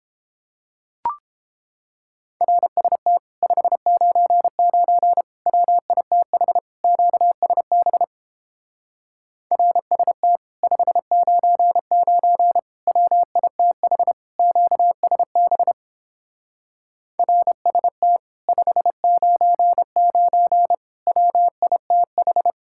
Morse code, heard as a keyed tone, about 29 wpm, 700 Hz. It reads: RST 599 WITH QSB RST 599 WITH QSB RST 599 WITH